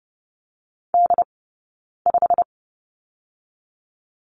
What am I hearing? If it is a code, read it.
Morse code, heard as a keyed tone, about 30 wpm, 700 Hz. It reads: D 5
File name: course / 30:10wpm / 086 Lesson 15d - Number 5 - Two Characters 30wpm